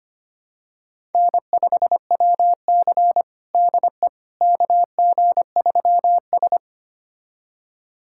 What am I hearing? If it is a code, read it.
Morse code, heard as a keyed tone, about 25 wpm, 700 Hz. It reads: N5WC DE KG3S
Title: USA Callsign DE Callsign - 4 Characters - 25wpm